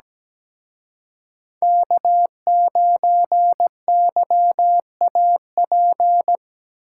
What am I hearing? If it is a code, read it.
Morse code, heard as a keyed tone, about 20 wpm, 700 Hz. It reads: K9YAP
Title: Speed Racing - USA 1x3 Call Signs 17wpm